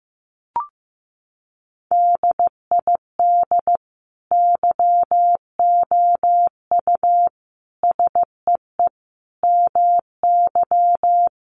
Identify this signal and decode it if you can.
Morse code, heard as a keyed tone, about 15 wpm, 700 Hz. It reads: DID YOU SEE MY
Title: Sentences from Top 1500 Words 15wpm